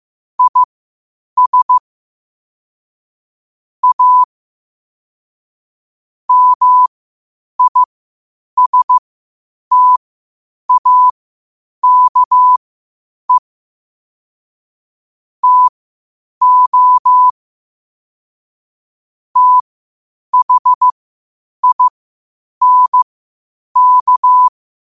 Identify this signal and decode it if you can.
Morse code, heard as a keyed tone, about 15 wpm, spaced as for 8 wpm, 1000 Hz. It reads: IS A MISTAKE TO THINK
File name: qotdfiles / 8wpm